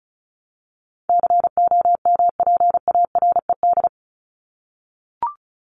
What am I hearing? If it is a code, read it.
Morse code, heard as a keyed tone, about 34 wpm, 700 Hz. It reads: COMPARED
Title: ED Suffix - Top 100 Words with Base Words 35wpm